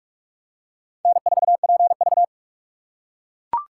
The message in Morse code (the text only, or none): N4PV